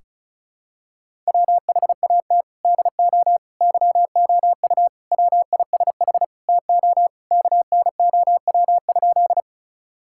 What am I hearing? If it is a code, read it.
Morse code, heard as a keyed tone, about 34 wpm, 700 Hz. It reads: WHAT DO YOU WISH TO KNOW?
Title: Sentences from Top 1600 Words 35wpm